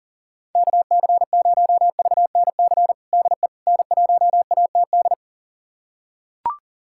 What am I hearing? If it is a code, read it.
Morse code, heard as a keyed tone, about 40 wpm, 700 Hz. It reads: KC0VNC DE N1ATD
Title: USA Callsign DE Callsign 40wpm